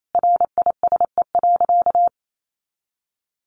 Morse code, heard as a keyed tone, about 28 wpm, 700 Hz. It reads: RISE.
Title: Sentences from Top 1300 Words 28wpm